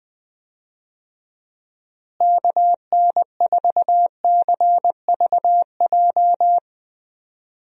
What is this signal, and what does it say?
Morse code, heard as a keyed tone, about 20 wpm, 700 Hz. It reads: KN4CVJ
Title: USA Call Signs - 6 Characters - Repeated 2x - 20wpm